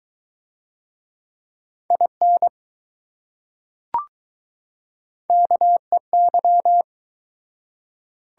Morse code, heard as a keyed tone, about 20 wpm, 700 Hz. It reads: IN KEY